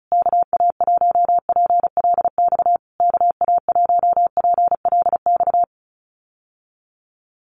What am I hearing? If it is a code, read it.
Morse code, heard as a keyed tone, about 34 wpm, 700 Hz. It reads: KA1PLX KA1PLX
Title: USA Call Signs Repeated 3x 35wpm